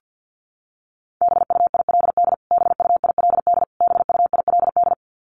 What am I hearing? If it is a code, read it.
Morse code, heard as a keyed tone, about 50 wpm, 700 Hz. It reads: BUILD BUILD BUILD
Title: Top 500 Words Repeated 50wpm